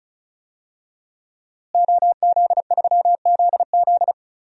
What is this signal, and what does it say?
Morse code, heard as a keyed tone, about 36 wpm, 700 Hz. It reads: OZ3ZZ
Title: DXCC-100 Call Signs - 5 Characters - Repeated 3x - 35wpm